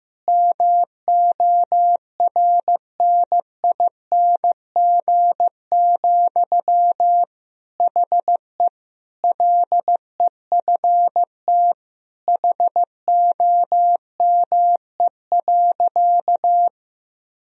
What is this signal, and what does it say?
Morse code, heard as a keyed tone, about 15 wpm, 700 Hz. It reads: MORNING, HE LEFT HOME.
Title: Sentences from Top 1000 Words - Review Part 4 - 15wpm